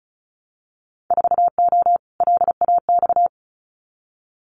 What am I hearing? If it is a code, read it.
Morse code, heard as a keyed tone, about 36 wpm, 700 Hz. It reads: 4O LAX
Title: Winter Field Day Exchanges Encore 35wpm